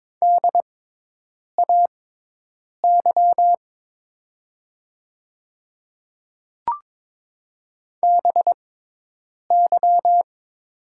Morse code, heard as a keyed tone, about 22 wpm, spaced as for 8 wpm, 700 Hz. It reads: DAY BY
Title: 6x Character Spacing - Top 100 Words - 22wpm